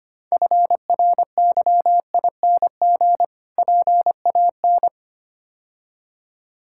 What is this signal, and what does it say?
Morse code, heard as a keyed tone, about 27 wpm, 700 Hz. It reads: FRYING PAN
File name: Difficult Words - Speed-Racing with Hints - 25wpm